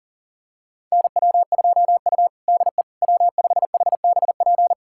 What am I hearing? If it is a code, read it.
Morse code, heard as a keyed tone, about 40 wpm, 700 Hz. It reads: NW2U DE W5HBP
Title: USA Callsign DE Callsign 40wpm